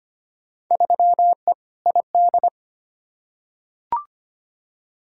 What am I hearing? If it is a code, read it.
Morse code, heard as a keyed tone, about 25 wpm, 700 Hz. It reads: 3E ID
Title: Field Day Exchanges Encore - 25wpm